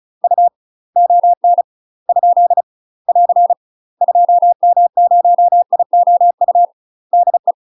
Morse code, heard as a keyed tone, about 36 wpm, 700 Hz. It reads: U ON ? <AR> 2M0IOU DE
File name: UK MASTER QSOs TRAINING / 35WPM / UK MASTER QSOs 0010